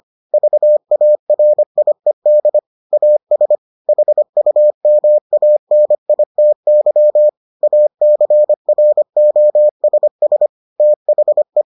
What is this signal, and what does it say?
Morse code, heard as a keyed tone, about 25 wpm, 600 Hz. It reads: VARIED AS HUMANITY ACROSS THE